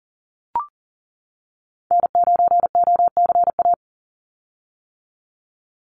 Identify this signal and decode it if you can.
Morse code, heard as a keyed tone, about 40 wpm, 700 Hz. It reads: N9OCA
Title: USA Call Signs 40wpm